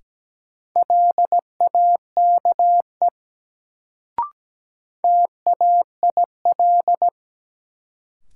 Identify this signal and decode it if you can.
Morse code, heard as a keyed tone, about 17 wpm, 700 Hz. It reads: LAKE TAIL
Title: Sentences from Top 1000 Words - Review Part 3 - New Word Drill - 17wpm